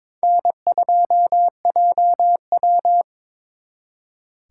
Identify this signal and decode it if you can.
Morse code, heard as a keyed tone, about 26 wpm, 700 Hz. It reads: N2JW
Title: Speed Racing - USA 1x2 Call Signs 22wpm